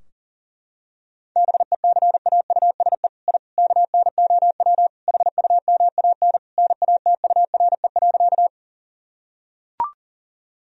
Morse code, heard as a keyed tone, about 37 wpm, 700 Hz. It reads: BECAUSE I KNOW HUMAN NATURE.
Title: Sentences from Top 1000 Words - Review Part 3 - 40wpm